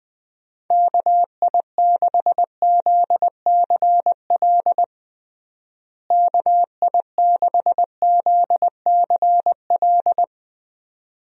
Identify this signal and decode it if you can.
Morse code, heard as a keyed tone, about 21 wpm, 700 Hz. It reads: KI6ZCL KI6ZCL
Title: USA Call Signs - 6 Characters - Repeated 2x - 20wpm